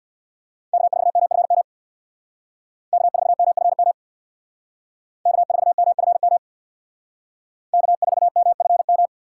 Morse code, heard as a keyed tone, about 60 wpm, 700 Hz. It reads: K4MFM K4MFM K4MFM K4MFM
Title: Speed Racing - USA 1x3 Call Signs 50wpm